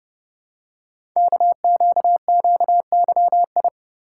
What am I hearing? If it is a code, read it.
Morse code, heard as a keyed tone, about 30 wpm, 700 Hz. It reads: KQQYI
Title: Code Groups - 5 Letter:Numbers - 30wpm